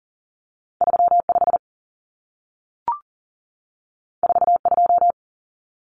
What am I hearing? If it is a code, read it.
Morse code, heard as a keyed tone, about 40 wpm, 700 Hz. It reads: 35 42
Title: Common Cut Numbers - 2-Digit 40wpm